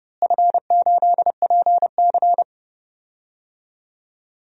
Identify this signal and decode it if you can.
Morse code, heard as a keyed tone, about 29 wpm, 700 Hz. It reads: F8PC
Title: DXCC-100 Call Signs - 4 Characters - Repeated 3x - 30wpm